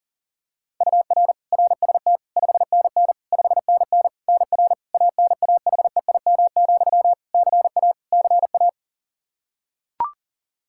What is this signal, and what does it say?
Morse code, heard as a keyed tone, about 37 wpm, 700 Hz. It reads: UR RST 5NN 5NN NR ANAHEIM, CA CA
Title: Practice QSO - Exchange 1 - Beginning - 40wpm